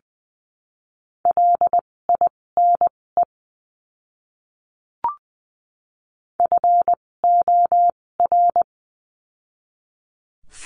Morse code, heard as a keyed tone, about 19 wpm, spaced as for 16 wpm, 700 Hz. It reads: LINE FOR
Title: 256 Lesson 42a - Words 12-20wpm